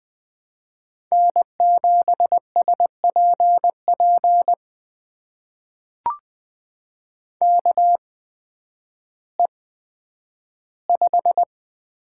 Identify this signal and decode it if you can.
Morse code, heard as a keyed tone, about 20 wpm, spaced as for 11 wpm, 700 Hz. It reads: N7SPP KE5
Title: USA Call Signs - 8x Character Spacing - 20wpm